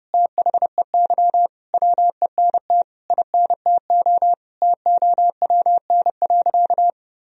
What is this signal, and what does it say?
Morse code, heard as a keyed tone, about 32 wpm, 700 Hz. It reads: THEY WENT INTO TOWN.
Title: Sentences from Top 1000 Words - Review Part 1 - 30wpm